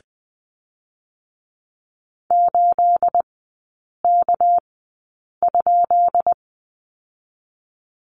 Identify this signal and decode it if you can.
Morse code, heard as a keyed tone, about 20 wpm, spaced as for 8 wpm, 700 Hz. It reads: 8K?
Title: 218 Lesson 35e - Number 0 - Three Characters 20wpm